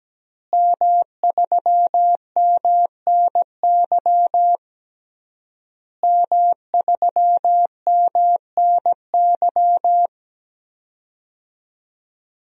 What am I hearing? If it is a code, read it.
Morse code, heard as a keyed tone, about 17 wpm, 700 Hz. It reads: M3MNY M3MNY
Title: DXCC-100 Call Signs - 5 Characters - Repeated 3x - 17wpm